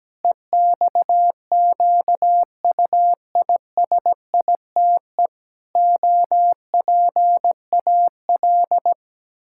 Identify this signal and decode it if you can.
Morse code, heard as a keyed tone, about 17 wpm, 700 Hz. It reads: EXQUISITE OPAL
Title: Pangram Challenge v2 - 17wpm